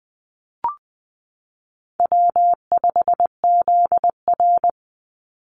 Morse code, heard as a keyed tone, about 20 wpm, 700 Hz. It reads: W5ZR
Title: Easy Call Signs Repeated 2x - 20wpm